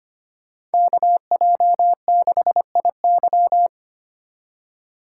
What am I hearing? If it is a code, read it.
Morse code, heard as a keyed tone, about 30 wpm, 700 Hz. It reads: KJ6IY
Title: Speed Racing - USA 2x2 Call Signs 25wpm